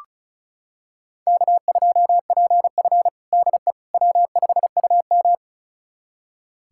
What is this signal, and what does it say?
Morse code, heard as a keyed tone, about 36 wpm, 700 Hz. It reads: K2PF DE W5UM
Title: USA Callsign DE Callsign - 4 Characters - 35wpm